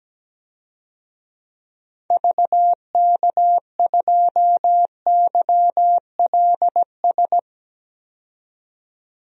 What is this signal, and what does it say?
Morse code, heard as a keyed tone, about 17 wpm, 700 Hz. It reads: VK2YLS
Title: DXCC-100 Call Signs - 6 Characters - 17wpm